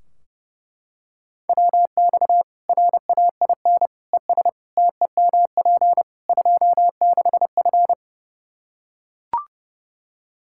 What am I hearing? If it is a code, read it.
Morse code, heard as a keyed tone, about 30 wpm, 700 Hz. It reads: WX RAIN ES TEMP 26F
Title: Practice QSO - Exchange 2 - Info - 30wpm